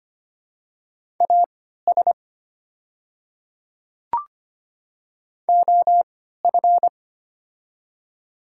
Morse code, heard as a keyed tone, about 25 wpm, spaced as for 14 wpm, 700 Hz. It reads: AS OF
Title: 3x Character Spacing - Top 100 Words - 25wpm